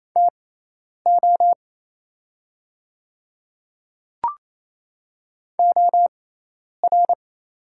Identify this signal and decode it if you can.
Morse code, heard as a keyed tone, about 28 wpm, spaced as for 10 wpm, 700 Hz. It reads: TO OR
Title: 6x Character Spacing - Top 200 Words - 28wpm